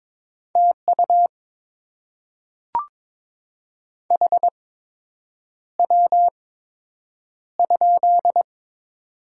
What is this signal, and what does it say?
Morse code, heard as a keyed tone, about 22 wpm, spaced as for 8 wpm, 700 Hz. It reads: TU HW?
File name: Top 20 QSO Elements - 8x Character Spacing - 22wpm